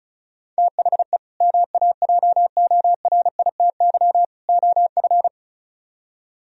Morse code, heard as a keyed tone, about 44 wpm, 700 Hz. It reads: THE MAJORITY OF